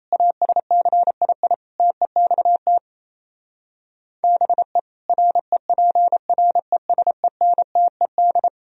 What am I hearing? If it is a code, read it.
Morse code, heard as a keyed tone, about 37 wpm, 700 Hz. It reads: ASCII TEXT BE REPRESENTED